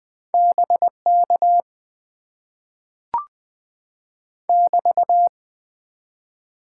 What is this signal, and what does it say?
Morse code, heard as a keyed tone, about 20 wpm, 700 Hz. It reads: BK <BT>
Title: Top 90 QSO Elements - 6x Character Spacing - 20wpm